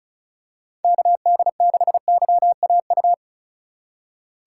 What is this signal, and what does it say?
Morse code, heard as a keyed tone, about 34 wpm, 700 Hz. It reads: KD6YAU
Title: USA Call Signs - 2x Character Spacing - 35wpm